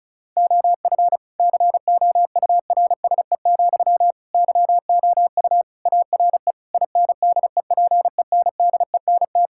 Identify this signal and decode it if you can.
Morse code, heard as a keyed tone, about 36 wpm, 700 Hz. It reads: OF COURSE, YOU ARE INDEPENDENT
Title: Sentences from Top 1000 Words - Review Part 6 - 35wpm